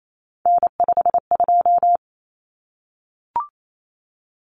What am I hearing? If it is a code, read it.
Morse code, heard as a keyed tone, about 28 wpm, 700 Hz. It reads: N52
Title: Common Cut Numbers - 3-Digit 28wpm